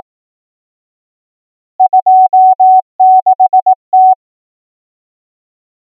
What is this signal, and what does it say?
Morse code, heard as a keyed tone, about 18 wpm, 750 Hz. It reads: 26T
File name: numbers / V2-morse